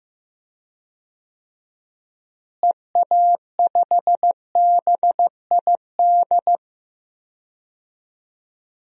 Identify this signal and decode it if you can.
Morse code, heard as a keyed tone, about 15 wpm, 700 Hz. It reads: EA5BID
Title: DXCC-100 Call Signs - 6 Characters - Repeated 2x - 15wpm